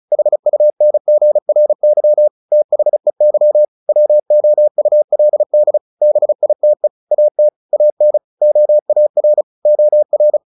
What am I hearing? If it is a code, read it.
Morse code, heard as a keyed tone, about 34 wpm, 600 Hz. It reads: HUNGRY THEY WOULD BITE AT AN OAR OR
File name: the old man and the sea 2 35wpm